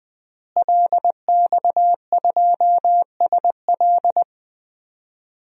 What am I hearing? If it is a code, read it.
Morse code, heard as a keyed tone, about 20 wpm, 700 Hz. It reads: LX2SL
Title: DXCC-100 Call Signs - 5 Characters - Repeated 3x - 20wpm